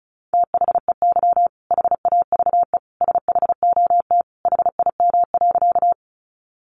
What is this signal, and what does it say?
Morse code, heard as a keyed tone, about 34 wpm, 700 Hz. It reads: THEY HAVE SHOT HIM.
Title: Sentences from Top 1100 Words 35wpm